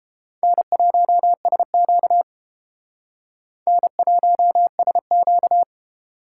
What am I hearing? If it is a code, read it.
Morse code, heard as a keyed tone, about 29 wpm, 700 Hz. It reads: N1SQ N1SQ